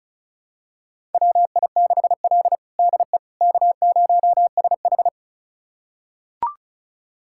Morse code, heard as a keyed tone, about 36 wpm, 700 Hz. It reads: WI6L DE K0SH
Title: USA Callsign DE Callsign - 4 Characters - 35wpm